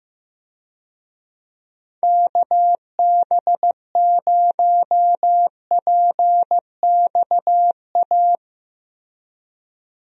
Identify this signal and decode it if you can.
Morse code, heard as a keyed tone, about 15 wpm, 700 Hz. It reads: KB0PXA
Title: USA Call Signs - 6 Characters - Repeated 3x - 15wpm